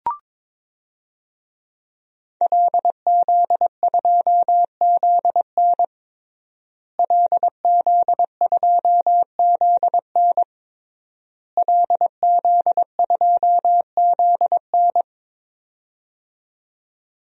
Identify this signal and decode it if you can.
Morse code, heard as a keyed tone, about 22 wpm, 700 Hz. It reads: LZ2ZN LZ2ZN LZ2ZN
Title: DXCC-100 Call Signs - 5 Characters - Repeated 3x - 22wpm